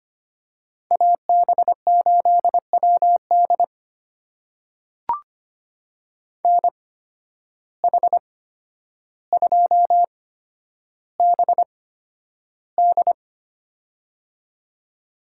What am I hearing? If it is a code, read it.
Morse code, heard as a keyed tone, about 25 wpm, spaced as for 11 wpm, 700 Hz. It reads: AB8WD NH2BD
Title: USA Call Signs - 5 Characters - 8x Character Spacing - 25wpm